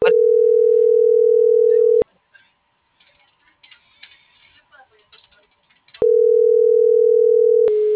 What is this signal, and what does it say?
Ringback tone (North America), 440 + 480 Hz.